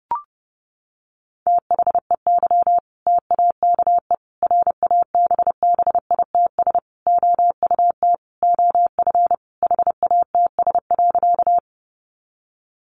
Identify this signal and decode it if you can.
Morse code, heard as a keyed tone, about 30 wpm, 700 Hz. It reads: THEY TAKE RABBITS OUT OF HATS.
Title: Easy Phrases 30wpm